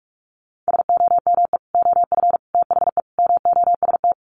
Morse code, heard as a keyed tone, about 44 wpm, 700 Hz. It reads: SOME OF THE MOST
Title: Sets of 4 Words 45wpm